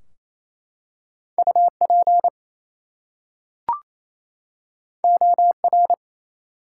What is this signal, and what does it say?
Morse code, heard as a keyed tone, about 28 wpm, 700 Hz. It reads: UP OR
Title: Two Letter Words - 28wpm